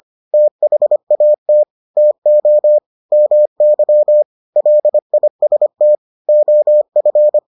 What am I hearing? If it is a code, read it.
Morse code, heard as a keyed tone, about 25 wpm, 600 Hz. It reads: THAT TO MY LIST OF